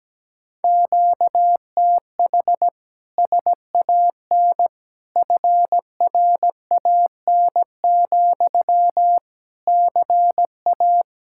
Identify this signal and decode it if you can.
Morse code, heard as a keyed tone, about 17 wpm, 700 Hz. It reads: QTH SAN FRAN, CA
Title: Speed-Racing Common QSO Phrases - 17wpm